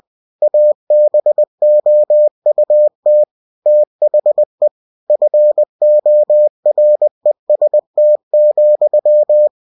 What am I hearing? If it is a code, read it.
Morse code, heard as a keyed tone, about 20 wpm, 600 Hz. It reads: ABOUT THE FOREST,
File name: cw-books / winnie-the-pooh / winnie-20wpm-page-0032